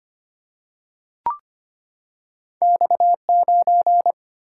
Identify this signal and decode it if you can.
Morse code, heard as a keyed tone, about 25 wpm, 700 Hz. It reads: X9
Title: Single-Letter Single-Number 25wpm